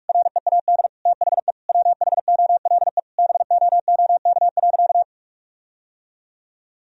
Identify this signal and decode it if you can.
Morse code, heard as a keyed tone, about 47 wpm, 700 Hz. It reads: READ THE WHOLE BOOK.
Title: Sentences from Top 1000 Words - Review Part 1 - 45wpm